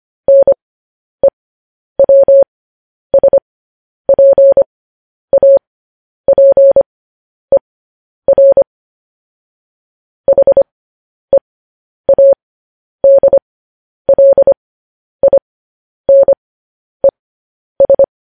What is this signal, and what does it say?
Morse code, heard as a keyed tone, about 25 wpm, spaced as for 10 wpm, 550 Hz. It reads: NEWSPAPER HEADLINES
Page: news-headlines-in-morse-code-at-10-wpm